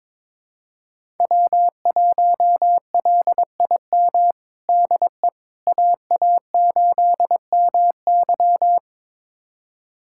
Morse code, heard as a keyed tone, about 22 wpm, 700 Hz. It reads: W1LIM DE AA8MY
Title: USA Callsign DE Callsign - 5 Characters - 22wpm